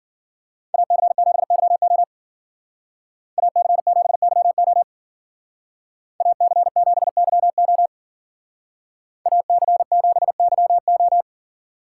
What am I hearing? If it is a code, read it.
Morse code, heard as a keyed tone, about 49 wpm, 700 Hz. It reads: AC7YO AC7YO AC7YO AC7YO
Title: Speed Racing - USA 2x2 Call Signs 40wpm